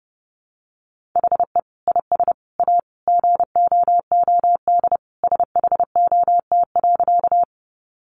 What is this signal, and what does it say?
Morse code, heard as a keyed tone, about 30 wpm, 700 Hz. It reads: HE IS A GOOD SHOT.
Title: Sentences from Top 1100 Words 30wpm